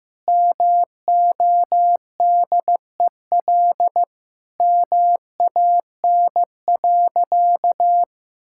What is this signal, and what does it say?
Morse code, heard as a keyed tone, about 15 wpm, 700 Hz. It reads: MODEL MAN.
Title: Sentences from Top 1000 Words - Review Part 5 - 15wpm